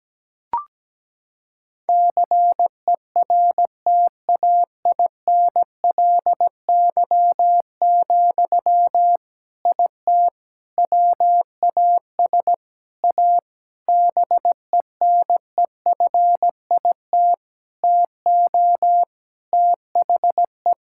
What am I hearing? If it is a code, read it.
Morse code, heard as a keyed tone, about 17 wpm, 700 Hz. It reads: CERTAINLY, IT WAS A BENEFIT TO THE
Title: Sentences from Top 1000 Words - Review Part 6 - 17wpm